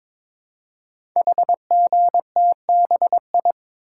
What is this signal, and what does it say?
Morse code, heard as a keyed tone, about 22 wpm, 690 Hz. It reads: HGTBI
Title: Code Groups - 5 Letters - 22wpm